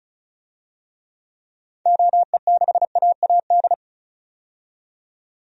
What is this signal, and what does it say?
Morse code, heard as a keyed tone, about 34 wpm, 700 Hz. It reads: OE6AAD
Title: DXCC-100 Call Signs - 6 Characters - 35wpm